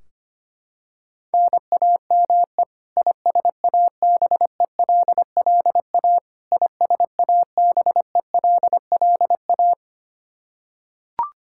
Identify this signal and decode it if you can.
Morse code, heard as a keyed tone, about 25 wpm, 700 Hz. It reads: NAME ISABELLA ISABELLA
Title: K1USN SST QSO Practice 25wpm